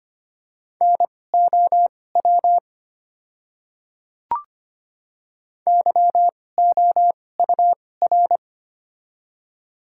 Morse code, heard as a keyed tone, about 25 wpm, spaced as for 18 wpm, 700 Hz. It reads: NOW YOUR
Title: 2x Character Spacing - Top 100 Words - 25wpm